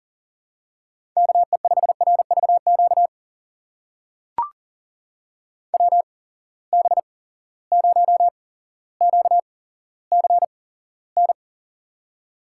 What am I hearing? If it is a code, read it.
Morse code, heard as a keyed tone, about 39 wpm, spaced as for 17 wpm, 700 Hz. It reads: KE5RVQ WB0QCN
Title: USA Call Signs - 6 Characters - 8x Character Spacing - 40wpm